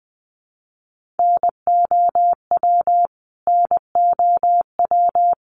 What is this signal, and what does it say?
Morse code, heard as a keyed tone, about 20 wpm, 700 Hz. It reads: NOW NOW
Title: Top 200 Words Repeated 20wpm